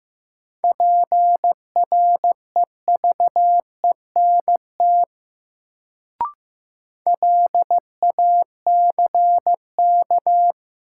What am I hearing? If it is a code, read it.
Morse code, heard as a keyed tone, about 15 wpm, 700 Hz. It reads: PREVENT LACK
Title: Sentences from Top 1100 Words - New Word Drill Alt - 15wpm